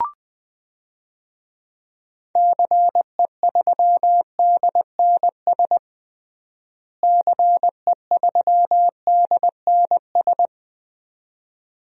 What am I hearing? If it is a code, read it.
Morse code, heard as a keyed tone, about 21 wpm, 700 Hz. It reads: CE3DNS CE3DNS
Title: DXCC-100 Call Signs - 6 Characters - Repeated 2x - 20wpm